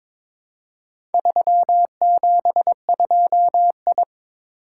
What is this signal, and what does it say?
Morse code, heard as a keyed tone, about 22 wpm, 700 Hz. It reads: 372I
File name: practice / 22wpm / Four Letter-Number Combinations - 22wpm